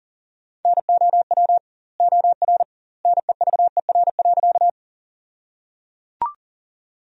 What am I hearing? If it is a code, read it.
Morse code, heard as a keyed tone, about 37 wpm, 700 Hz. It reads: NOW OR NEVER.
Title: 2x Word-Spacing - Sentences from Top 300 Words - 40wpm